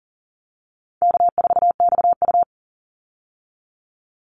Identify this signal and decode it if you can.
Morse code, heard as a keyed tone, about 39 wpm, 700 Hz. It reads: K4XU
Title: Easy Call Signs - 40wpm